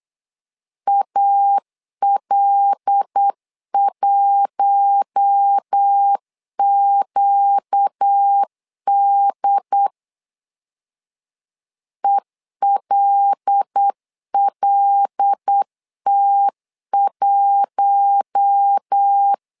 Morse code, heard as a keyed tone, about 8 wpm, 800 Hz. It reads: AL1QD ELLT1
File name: LEZIONI / 5 LBVC / 5 L 8wpm 2831